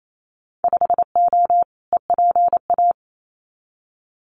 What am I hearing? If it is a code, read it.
Morse code, heard as a keyed tone, about 28 wpm, 700 Hz. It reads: 5O EPA
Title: Winter Field Day Exchanges Encore 28wpm